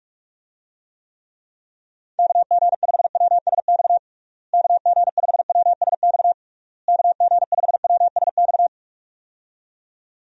Morse code, heard as a keyed tone, about 46 wpm, 700 Hz. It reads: KG5WSX KG5WSX KG5WSX
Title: USA Call Signs - 6 Characters - Repeated 3x - 45wpm